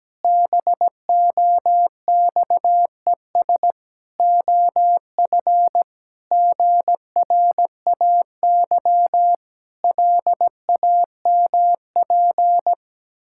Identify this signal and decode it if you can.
Morse code, heard as a keyed tone, about 17 wpm, 700 Hz. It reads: BOXES OF GRAY LAMP
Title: Pangram Challenge v2 - 17wpm